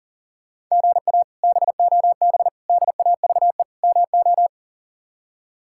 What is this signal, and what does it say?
Morse code, heard as a keyed tone, about 40 wpm, 700 Hz. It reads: GA BOB DAVE MO
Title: K1USN SST QSO Practice - 40wpm